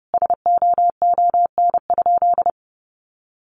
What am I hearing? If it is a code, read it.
Morse code, heard as a keyed tone, about 30 wpm, 700 Hz. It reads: SOON?
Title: Sentences from Top 1300 Words 30wpm